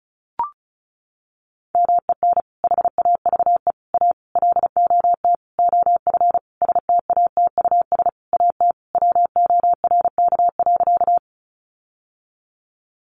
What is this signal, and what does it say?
Morse code, heard as a keyed tone, about 34 wpm, 700 Hz. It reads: MEN HAVE A LOT OF STATUS AT WORK.
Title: Sentences from Top 1400 Words 35wpm